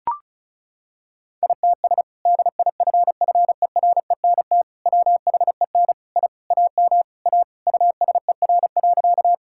Morse code, heard as a keyed tone, about 36 wpm, 700 Hz. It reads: ITS DIFFERENT WHEN I AM A USER.
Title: Sentences from Top 1000 Words - Review Part 5 - 35wpm